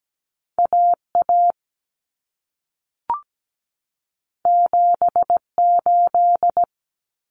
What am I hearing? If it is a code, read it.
Morse code, heard as a keyed tone, about 17 wpm, 700 Hz. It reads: AA 78
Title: Common Cut Numbers - 2-Digit 17wpm